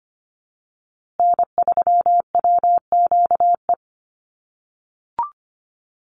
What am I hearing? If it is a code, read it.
Morse code, heard as a keyed tone, about 25 wpm, 700 Hz. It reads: N3WQE
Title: USA Call Signs Repeated 2x 25wpm